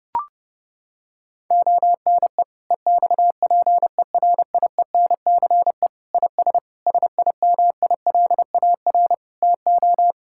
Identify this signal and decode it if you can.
Morse code, heard as a keyed tone, about 30 wpm, 700 Hz. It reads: ONE EXPERIENCE IS SIMILAR TO
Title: Sentences from Top 1000 Words - Review Part 5 - 30wpm